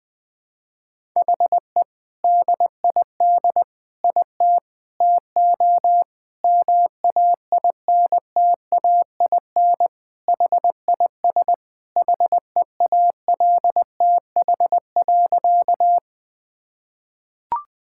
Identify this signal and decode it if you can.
Morse code, heard as a keyed tone, about 21 wpm, 700 Hz. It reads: HE DID IT TO MAINTAIN HIS HEALTH.
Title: Sentences from Top 1000 Words - Review Part 5 - 20wpm